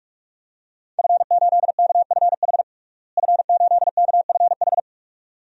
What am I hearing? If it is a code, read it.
Morse code, heard as a keyed tone, about 46 wpm, 700 Hz. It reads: F8KFH F8KFH
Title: DXCC-100 Call Signs - 5 Characters - Repeated 2x - 45wpm